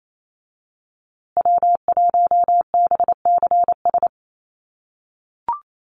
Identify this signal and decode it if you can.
Morse code, heard as a keyed tone, about 28 wpm, 700 Hz. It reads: W1BCS
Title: USA Call Signs Repeated 3x 28wpm